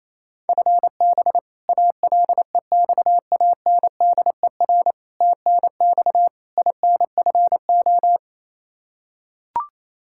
Morse code, heard as a keyed tone, about 28 wpm, 700 Hz. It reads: FB ALEXANDER TNX INFO
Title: K1USN SST QSO Practice 28wpm